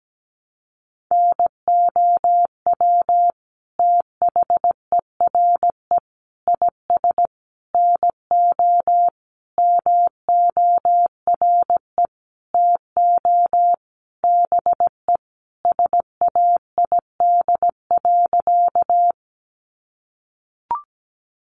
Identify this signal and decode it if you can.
Morse code, heard as a keyed tone, about 17 wpm, 700 Hz. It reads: NOW THERE IS NO MORE TO BE SAID.
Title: Sentences from Top 100 Words 17wpm